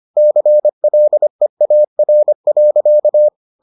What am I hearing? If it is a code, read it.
Morse code, heard as a keyed tone, about 25 wpm, 600 Hz. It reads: CLEAR.